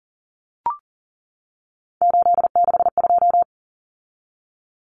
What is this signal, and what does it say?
Morse code, heard as a keyed tone, about 39 wpm, 700 Hz. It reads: 862
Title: Common Cut Numbers - 3-Digit 40wpm